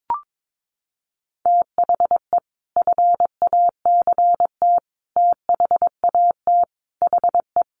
Morse code, heard as a keyed tone, about 22 wpm, 700 Hz. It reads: THE FACT THAT HE
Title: Sets of 5 Words 22wpm